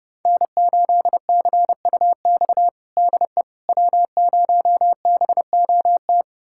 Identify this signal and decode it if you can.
Morse code, heard as a keyed tone, about 32 wpm, 700 Hz. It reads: N8CUX DE W0BOT
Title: USA Callsign DE Callsign - 5 Characters - 30wpm